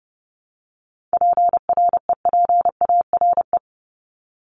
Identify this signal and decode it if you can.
Morse code, heard as a keyed tone, about 30 wpm, 700 Hz. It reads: PREPARE
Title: ED Suffix - Top 100 Words with Base Words 30wpm